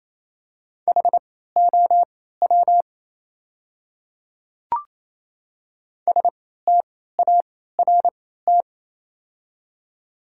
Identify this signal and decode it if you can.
Morse code, heard as a keyed tone, about 28 wpm, spaced as for 16 wpm, 700 Hz. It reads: HOW START
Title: 3x Character Spacing - Top 300 Words - 28wpm